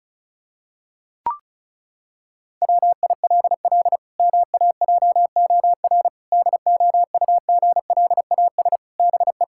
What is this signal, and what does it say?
Morse code, heard as a keyed tone, about 36 wpm, 700 Hz. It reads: WILL MAJOR DOUGLAS BE
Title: Pangram Challenge v2 - 35wpm